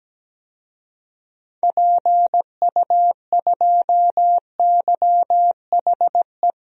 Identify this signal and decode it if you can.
Morse code, heard as a keyed tone, about 17 wpm, 700 Hz. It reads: PU2YHE
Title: DXCC-100 Call Signs - 6 Characters - Repeated 2x - 17wpm